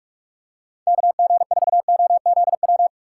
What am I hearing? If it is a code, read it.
Morse code, heard as a keyed tone, about 46 wpm, 700 Hz. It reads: KG4OZW
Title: USA Call Signs - 6 Characters - 6x Character Spacing - 45wpm